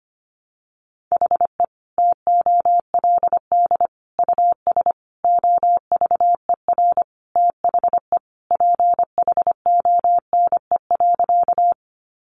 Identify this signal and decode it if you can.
Morse code, heard as a keyed tone, about 25 wpm, 700 Hz. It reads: HE TOLD US OVER THE PHONE.
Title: Sentences from Top 1100 Words 25wpm